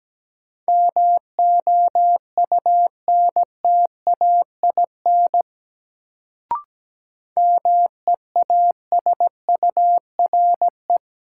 Morse code, heard as a keyed tone, about 17 wpm, 700 Hz. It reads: MOUNTAIN MEASURE
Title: Sentences from Top 1000 Words - Review Part 1 - New Word Drill - 17wpm